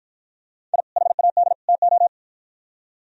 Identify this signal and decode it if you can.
Morse code, heard as a keyed tone, about 53 wpm, 700 Hz. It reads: I HAD TO